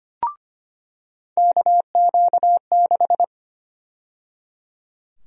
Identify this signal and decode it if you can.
Morse code, heard as a keyed tone, about 25 wpm, 700 Hz. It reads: KQ6
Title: USA Call Sign Prefixes - 25wpm